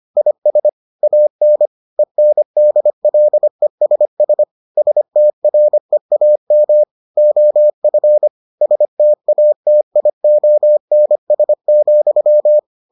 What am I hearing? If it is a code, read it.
Morse code, heard as a keyed tone, about 25 wpm, 600 Hz. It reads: IS AN ENDLESS STREAM OF STATIONS,